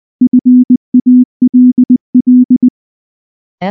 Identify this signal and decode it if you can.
Morse code, heard as a keyed tone, about 19 wpm, 250 Hz. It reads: FALL
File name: morse / morse-training-20WPM-3